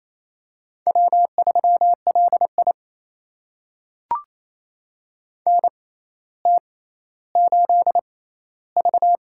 Morse code, heard as a keyed tone, about 28 wpm, spaced as for 14 wpm, 700 Hz. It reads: W3LI NT8V
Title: USA Call Signs - 4 Characters - 6x Character Spacing - 28wpm